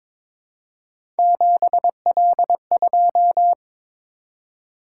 Morse code, heard as a keyed tone, about 22 wpm, 700 Hz. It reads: 7L2
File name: practice / 22wpm / Three Letter-Number Combinations - 22wpm